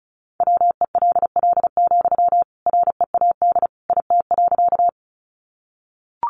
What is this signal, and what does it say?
Morse code, heard as a keyed tone, about 34 wpm, 700 Hz. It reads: WELL, READ IT.
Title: Sentences from Top 200 Words 35wpm